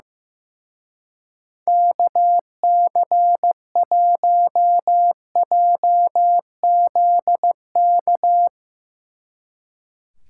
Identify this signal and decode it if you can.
Morse code, heard as a keyed tone, about 16 wpm, 700 Hz. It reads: KC1JZK
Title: Speed Racing - USA Call Signs 15wpm